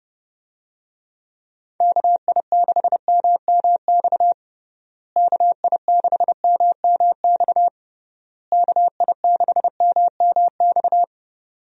Morse code, heard as a keyed tone, about 32 wpm, 700 Hz. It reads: KI6MMX KI6MMX KI6MMX
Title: USA Call Signs - 6 Characters - Repeated 3x - 30wpm